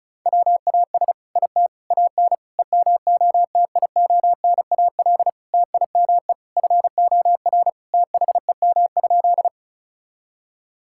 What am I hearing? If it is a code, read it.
Morse code, heard as a keyed tone, about 36 wpm, 700 Hz. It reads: WAS IT AN EMOTIONAL TIME FOR THEM?
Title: Sentences from Top 1000 Words - Review Part 5 - 35wpm